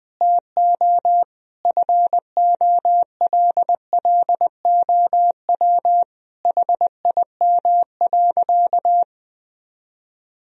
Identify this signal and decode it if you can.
Morse code, heard as a keyed tone, about 21 wpm, 700 Hz. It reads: TO FOLLOW HIM.
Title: Sentences from Top 1000 Words - Review Part 3 - 20wpm